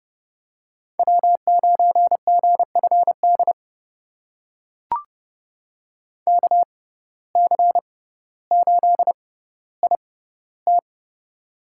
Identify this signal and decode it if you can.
Morse code, heard as a keyed tone, about 30 wpm, spaced as for 15 wpm, 700 Hz. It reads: W9GFD KC8IT
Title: USA Call Signs - 5 Characters - 6x Character Spacing - 30wpm